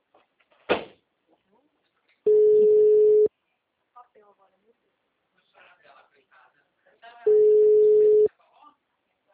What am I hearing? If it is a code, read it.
Ringback tone (Europe, 425 Hz), with speech.
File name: out- -2049- - - .10647